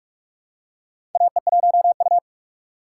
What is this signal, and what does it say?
Morse code, heard as a keyed tone, about 44 wpm, 700 Hz. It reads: AE1U
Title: USA Call Signs - 3x Character Spacing - 45wpm